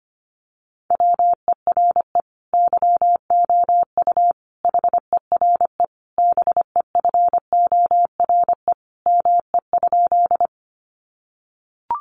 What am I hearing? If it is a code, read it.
Morse code, heard as a keyed tone, about 25 wpm, 700 Hz. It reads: WERE YOU HERE BEFORE ME?
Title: Sentences from Top 200 Words 25wpm